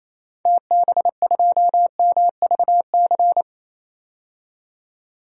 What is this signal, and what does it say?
Morse code, heard as a keyed tone, about 28 wpm, 700 Hz. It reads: TB2MVC
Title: DXCC-100 Call Signs - 6 Characters - 28wpm